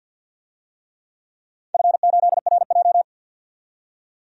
Morse code, heard as a keyed tone, about 48 wpm, 700 Hz. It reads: F8RJ
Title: DXCC-100 Call Signs - 4 Characters - 50wpm